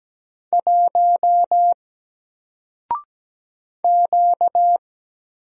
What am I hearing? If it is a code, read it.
Morse code, heard as a keyed tone, about 17 wpm, 700 Hz. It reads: 1 Q
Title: Single Letter-Number Alt 17wpm